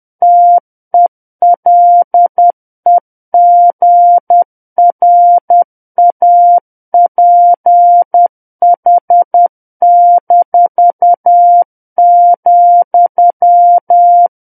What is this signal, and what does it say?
Morse code, heard as a keyed tone, about 10 wpm, 700 Hz. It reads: TELEGRAPH-,
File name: Book One / 01-Chapter 1 - 10 WPM